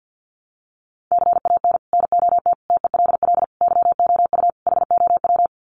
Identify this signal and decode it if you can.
Morse code, heard as a keyed tone, about 49 wpm, 700 Hz. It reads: CAN NOT TELL YOU HOW
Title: Sets of 5 Words Encore2 50wpm